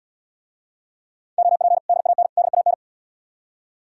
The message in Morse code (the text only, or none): OP DAN DAN